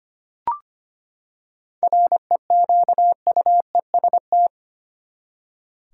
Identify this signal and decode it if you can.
Morse code, heard as a keyed tone, about 25 wpm, 700 Hz. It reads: REQUEST
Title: Seven Letter Words - 25wpm